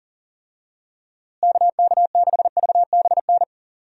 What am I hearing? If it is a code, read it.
Morse code, heard as a keyed tone, about 40 wpm, 700 Hz. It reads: KK6VBN
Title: USA Call Signs - 6 Characters - Repeated 3x - 40wpm